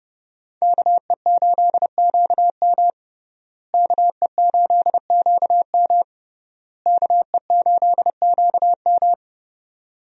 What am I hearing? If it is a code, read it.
Morse code, heard as a keyed tone, about 30 wpm, 700 Hz. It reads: KE8QM KE8QM KE8QM
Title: USA Call Signs - 5 Characters - Repeated 3x - 30wpm